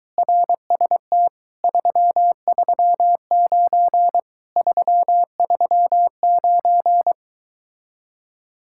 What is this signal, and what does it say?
Morse code, heard as a keyed tone, about 23 wpm, 700 Hz. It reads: RST 339 339